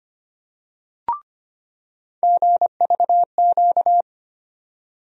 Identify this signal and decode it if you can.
Morse code, heard as a keyed tone, about 25 wpm, 700 Hz. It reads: GVQ
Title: USA Call Sign Postfixes - 3 Letters 25wpm